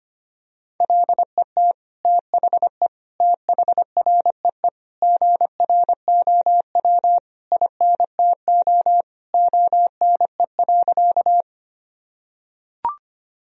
Morse code, heard as a keyed tone, about 25 wpm, 700 Hz. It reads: LET THE THREE GROW INTO ONE.
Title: Sentences from Top 1000 Words - Review Part 1 - 25wpm